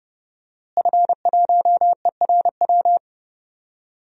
Morse code, heard as a keyed tone, about 30 wpm, 700 Hz. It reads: F1ERW
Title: DXCC-100 Call Signs - 5 Characters - Repeated 2x - 30wpm